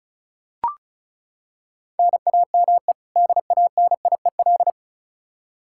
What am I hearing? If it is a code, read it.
Morse code, heard as a keyed tone, about 34 wpm, 700 Hz. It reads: NAME DANIEL
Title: Practice QSO - Exchange 1 - Beginning - 35wpm